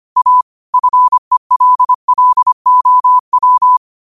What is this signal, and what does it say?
Morse code, heard as a keyed tone, about 25 wpm, 1000 Hz. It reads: A FELLOW